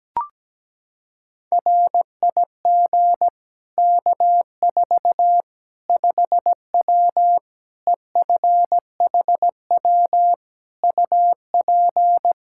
Morse code, heard as a keyed tone, about 17 wpm, 700 Hz. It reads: RIG K4 5W EFHW UP
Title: Practice QSO - Exchange 2 - Info - 17wpm